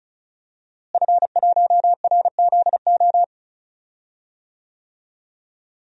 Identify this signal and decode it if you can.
Morse code, heard as a keyed tone, about 36 wpm, 700 Hz. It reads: F1RZO
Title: DXCC-100 Call Signs - 5 Characters - Repeated 2x - 35wpm